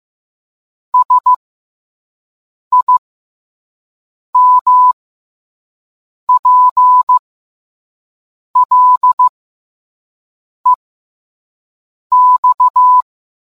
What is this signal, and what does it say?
Morse code, heard as a keyed tone, about 15 wpm, spaced as for 5 wpm, 1000 Hz. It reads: SIMPLEX